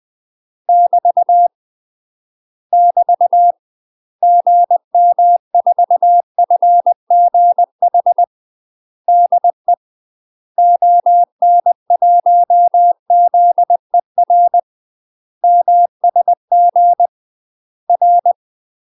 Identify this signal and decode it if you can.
Morse code, heard as a keyed tone, about 20 wpm, 700 Hz. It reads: <BT> <BT> GM4FGH DE ON1ZER MSG R
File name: UK MASTER QSOs TRAINING / 20WPM / UK MASTER QSOs 0042